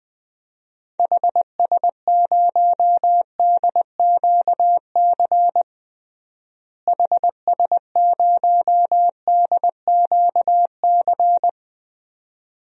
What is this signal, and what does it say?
Morse code, heard as a keyed tone, about 21 wpm, 700 Hz. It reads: HS0DQC HS0DQC
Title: DXCC-100 Call Signs - 6 Characters - Repeated 2x - 20wpm